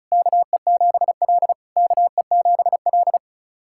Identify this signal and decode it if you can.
Morse code, heard as a keyed tone, about 34 wpm, 690 Hz. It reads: KE7L KE7L
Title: SOTA QSO Practice 35wpm